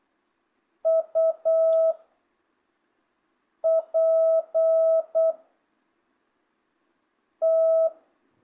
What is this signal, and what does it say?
Morse code, heard as a keyed tone, about 8 wpm, 660 Hz. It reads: U P T
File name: jh8cbh / 2025cw